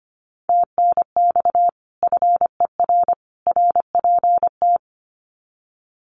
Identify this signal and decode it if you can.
Morse code, heard as a keyed tone, about 25 wpm, 700 Hz. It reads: TNX FER RPT
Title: Common QSO Phrases 25wpm